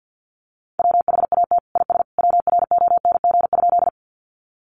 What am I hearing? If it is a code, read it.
Morse code, heard as a keyed tone, about 49 wpm, 700 Hz. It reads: WHAT IS WRONG?
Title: Sentences from Top 900 Words 50wpm